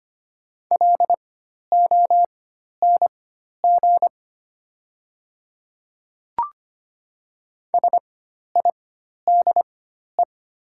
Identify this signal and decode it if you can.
Morse code, heard as a keyed tone, about 25 wpm, spaced as for 12 wpm, 700 Hz. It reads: LONG SIDE
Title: 4x Character Spacing - Top 100 Words - 25wpm